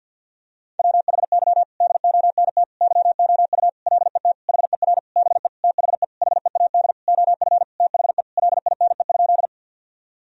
Morse code, heard as a keyed tone, about 53 wpm, 700 Hz. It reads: WHY DONT YOU LET HER BE THE HEAD OF THE LINE?
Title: Sentences from Top 1000 Words - Review Part 1 - 50wpm